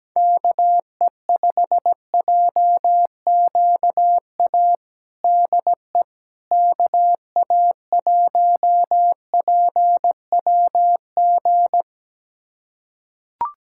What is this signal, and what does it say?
Morse code, heard as a keyed tone, about 17 wpm, 700 Hz. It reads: KE5JQA DE KA1PWG
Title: USA Callsign DE Callsign 17wpm